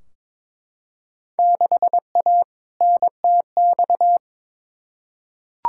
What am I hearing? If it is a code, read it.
Morse code, heard as a keyed tone, about 21 wpm, 700 Hz. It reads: 6A NTX
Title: Field Day Exchanges - 22wpm